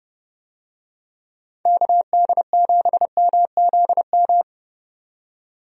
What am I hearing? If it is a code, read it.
Morse code, heard as a keyed tone, about 30 wpm, 700 Hz. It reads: KD7MZM
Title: USA Call Signs - 6 Characters - 30wpm